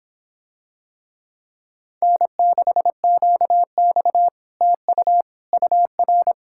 Morse code, heard as a keyed tone, about 26 wpm, 700 Hz. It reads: N6QX TU UR